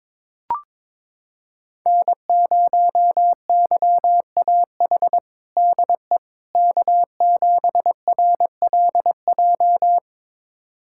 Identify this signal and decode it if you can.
Morse code, heard as a keyed tone, about 22 wpm, 700 Hz. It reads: N0YAH DE K7RLJ
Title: USA Callsign DE Callsign - 5 Characters - 22wpm